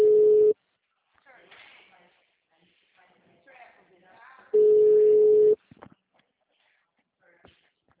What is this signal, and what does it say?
Ringback tone (Europe, 425 Hz), with speech.
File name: out- -2043- - - .6996